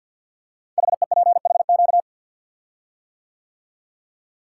Morse code, heard as a keyed tone, about 49 wpm, 700 Hz. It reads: HEPHQ
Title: Code Groups - 5 Letters - 50wpm